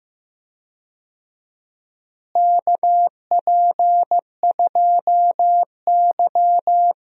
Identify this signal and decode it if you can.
Morse code, heard as a keyed tone, about 15 wpm, 700 Hz. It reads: KP2Y
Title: DXCC-100 Call Signs - 4 Characters - Repeated 2x - 15wpm